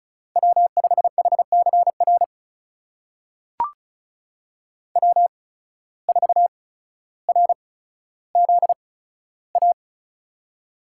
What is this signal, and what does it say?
Morse code, heard as a keyed tone, about 36 wpm, spaced as for 15 wpm, 700 Hz. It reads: W5HCR W4RZA
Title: USA Call Signs - 5 Characters - 8x Character Spacing - 35wpm